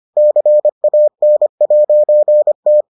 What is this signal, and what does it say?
Morse code, heard as a keyed tone, about 25 wpm, 600 Hz. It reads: CAN'T